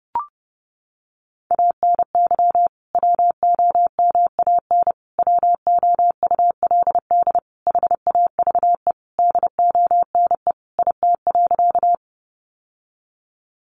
Morse code, heard as a keyed tone, about 30 wpm, 700 Hz. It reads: ANY WOMAN WOULD HAVE DONE IT.
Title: Sentences from Top 1100 Words 30wpm